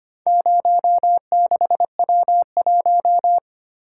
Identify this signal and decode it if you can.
Morse code, heard as a keyed tone, about 25 wpm, 700 Hz. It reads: 06W1
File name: practice / 25wpm / Four Letter-Number Combinations - 25wpm